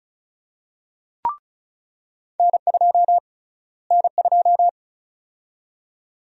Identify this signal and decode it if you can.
Morse code, heard as a keyed tone, about 36 wpm, 700 Hz. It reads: N2 N2
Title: USA Call Sign Prefixes - Repeated 2x - 35wpm